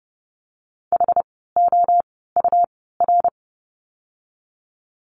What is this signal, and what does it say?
Morse code, heard as a keyed tone, about 32 wpm, spaced as for 17 wpm, 700 Hz. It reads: HOUR
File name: course / 30:10wpm / 108 Lesson 18f - Letter m - Words 30wpm